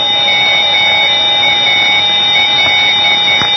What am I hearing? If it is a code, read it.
Morse code, heard as a keyed tone, about 10 wpm, 2250 Hz, with noise. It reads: MAL
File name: ELB NDB 1989